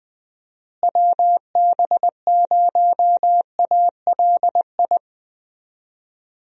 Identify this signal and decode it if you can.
Morse code, heard as a keyed tone, about 20 wpm, 700 Hz. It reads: WB0ALI
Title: USA Call Signs - 6 Characters - Repeated 2x - 20wpm